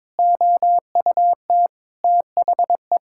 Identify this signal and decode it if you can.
Morse code, heard as a keyed tone, about 22 wpm, 700 Hz. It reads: OUT THE
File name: practice / 22wpm / Sentences with 8-Word Phrases - 22wpm